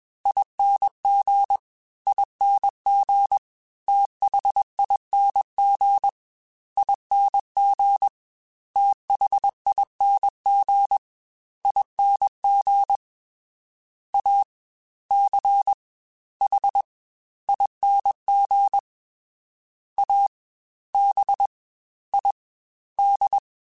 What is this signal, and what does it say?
Morse code, heard as a keyed tone, about 21 wpm, 800 Hz. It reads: ING ING THING ING THING ING A C H ING A B I D
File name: Special INGs-mp3s / Special INGs-mp3s / Fast ing @20